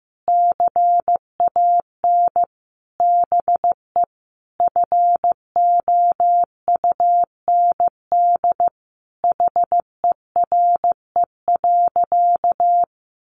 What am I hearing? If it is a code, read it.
Morse code, heard as a keyed tone, about 15 wpm, 700 Hz. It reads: CAN BE FOUND HERE.
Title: Sentences from Top 1500 Words 15wpm